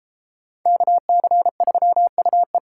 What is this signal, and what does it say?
Morse code, heard as a keyed tone, about 36 wpm, 700 Hz. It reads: KC3UE